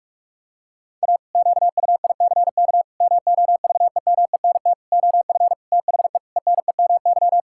Morse code, heard as a keyed tone, about 46 wpm, 700 Hz. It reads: A QUICK MOVEMENT OF THE ENEMY
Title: Pangram Challenge v2 - 45wpm